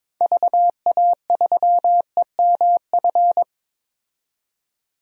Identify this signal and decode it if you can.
Morse code, heard as a keyed tone, about 21 wpm, 700 Hz. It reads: VA3EMF
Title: DXCC-100 Call Signs - 6 Characters - 22wpm